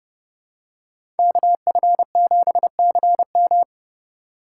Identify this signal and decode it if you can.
Morse code, heard as a keyed tone, about 30 wpm, 700 Hz. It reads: KF7CM
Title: USA Call Signs - 5 Characters - 3x Character Spacing - 30wpm